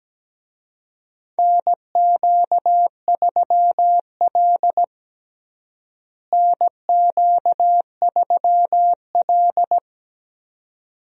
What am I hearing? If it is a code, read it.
Morse code, heard as a keyed tone, about 17 wpm, 700 Hz. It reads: NQ3L NQ3L
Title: USA Call Signs - 4 Characters - Repeated 2x - 17wpm